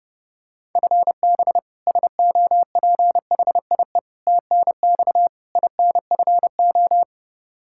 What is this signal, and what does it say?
Morse code, heard as a keyed tone, about 30 wpm, 700 Hz. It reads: FB SOPHIE TNX INFO
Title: K1USN SST QSO Practice 30wpm